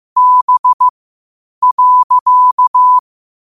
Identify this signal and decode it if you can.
Morse code, heard as a keyed tone, about 15 wpm, spaced as for 8 wpm, 1000 Hz. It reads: B.
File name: qotdfiles / 8wpm